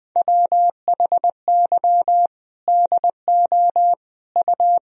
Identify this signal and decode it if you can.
Morse code, heard as a keyed tone, about 19 wpm, 700 Hz. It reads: WHY DO U
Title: Sentences from Top 200 Words and Abbreviations 20wpm